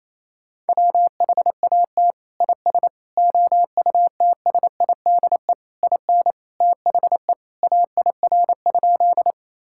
Morse code, heard as a keyed tone, about 28 wpm, 700 Hz. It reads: WHAT IS OUTSIDE IN THE AIR?
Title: Sentences from Top 1000 Words - Review Part 6 - 28wpm